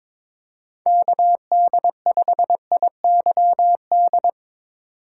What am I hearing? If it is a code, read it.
Morse code, heard as a keyed tone, about 22 wpm, 700 Hz. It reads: KD5IYD
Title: USA Call Signs - 6 Characters - 4x Character Spacing - 22wpm